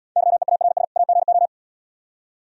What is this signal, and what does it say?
Morse code, heard as a keyed tone, about 76 wpm, 700 Hz. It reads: YEARS AGO